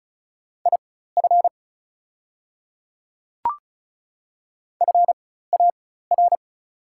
Morse code, heard as a keyed tone, about 36 wpm, spaced as for 16 wpm, 700 Hz. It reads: IF FAR
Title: 4x Character Spacing - Top 300 Words - 35wpm